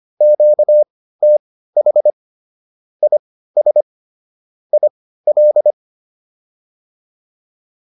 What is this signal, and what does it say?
Morse code, heard as a keyed tone, about 25 wpm, spaced as for 15 wpm, 600 Hz. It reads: QTH IS IL